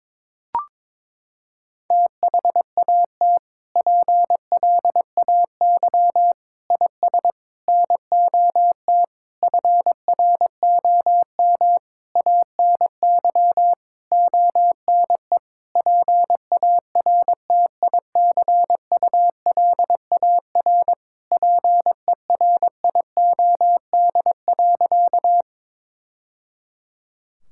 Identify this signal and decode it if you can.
Morse code, heard as a keyed tone, about 21 wpm, 700 Hz. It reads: THAT PLAY IS NOT FROM ANY ONE PARTICULAR PERIOD.
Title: Sentences from Top 1000 Words - Review Part 5 - 22wpm